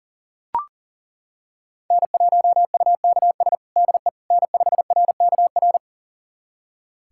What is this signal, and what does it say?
Morse code, heard as a keyed tone, about 37 wpm, 700 Hz. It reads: N1UKS DE N5RKR
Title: USA Callsign DE Callsign - 5 Characters - 40wpm